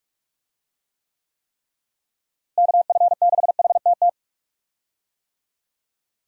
Morse code, heard as a keyed tone, about 46 wpm, 700 Hz. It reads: KF6HTT
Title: USA Call Signs - 6 Characters - Repeated 3x - 45wpm